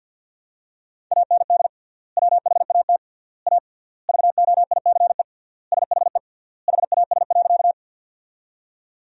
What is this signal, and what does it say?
Morse code, heard as a keyed tone, about 47 wpm, 700 Hz. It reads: AND WHAT A VOICE SHE HAS.
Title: 3x Word-Spacing - Sentences from Top 500 Words - 50wpm